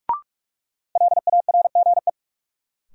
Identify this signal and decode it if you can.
Morse code, heard as a keyed tone, about 46 wpm, 700 Hz. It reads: LARGE
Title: Sets of 4 Words - New Word Drill - 45wpm